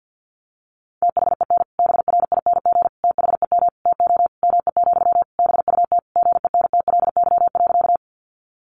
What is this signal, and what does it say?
Morse code, heard as a keyed tone, about 50 wpm, 700 Hz. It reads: THEN BRING THEM TO ME, BUT GENTLY.
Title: Sentences from Top 1500 Words 50wpm